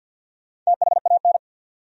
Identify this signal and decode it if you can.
Morse code, heard as a keyed tone, about 66 wpm, spaced as for 58 wpm, 700 Hz. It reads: THAN
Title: Speed-Racing Top 200 Words 50wpm